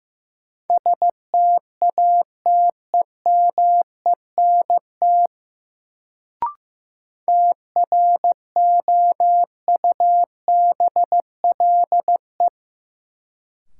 Morse code, heard as a keyed tone, about 15 wpm, 700 Hz. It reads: STATEMENT TROUBLE
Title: Sentences from Top 1200 Words - New Word Drill Alt - 15wpm